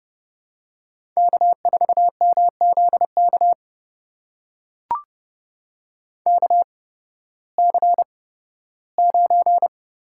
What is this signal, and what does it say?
Morse code, heard as a keyed tone, about 30 wpm, spaced as for 16 wpm, 700 Hz. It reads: K4MZK KC9
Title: USA Call Signs - 8x Character Spacing - 30wpm